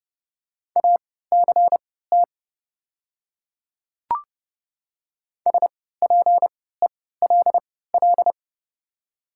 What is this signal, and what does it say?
Morse code, heard as a keyed tone, about 29 wpm, spaced as for 17 wpm, 700 Hz. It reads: ACT SPELL
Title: 3x Character Spacing - Top 200 Words - 30wpm